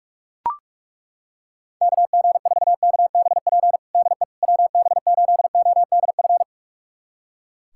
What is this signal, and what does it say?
Morse code, heard as a keyed tone, about 41 wpm, 700 Hz. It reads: KG4KBP DE WB8ODF
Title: USA Callsign DE Callsign 45wpm